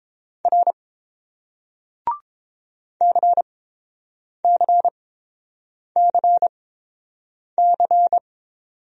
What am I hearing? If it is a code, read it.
Morse code, heard as a keyed tone, about 30 wpm, spaced as for 7 wpm, 700 Hz. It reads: R CCCC